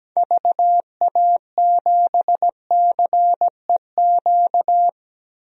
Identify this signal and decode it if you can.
Morse code, heard as a keyed tone, about 17 wpm, 700 Hz. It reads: VA7CEQ
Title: DXCC-100 Call Signs - 6 Characters - 17wpm